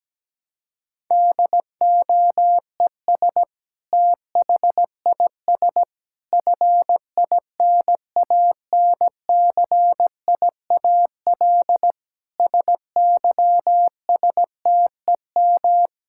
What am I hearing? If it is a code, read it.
Morse code, heard as a keyed tone, about 17 wpm, 700 Hz. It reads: DOES THIS FINANCIAL SYSTEM
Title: Sentences from Top 1000 Words - Review Part 5 - 17wpm